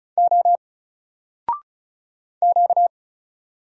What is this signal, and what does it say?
Morse code, heard as a keyed tone, about 34 wpm, 700 Hz. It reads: O Q
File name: Single-Letter-Number-35wpm